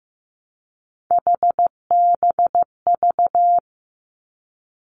Morse code, heard as a keyed tone, about 15 wpm, 700 Hz. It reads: HBV
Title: Three Challenging Letter-Number Combinations 15wpm